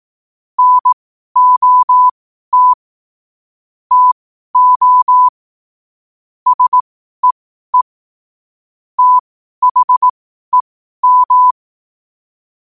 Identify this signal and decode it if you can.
Morse code, heard as a keyed tone, about 18 wpm, spaced as for 12 wpm, 1000 Hz. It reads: NOT TO SEE THEM
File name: qotdfiles / 12wpm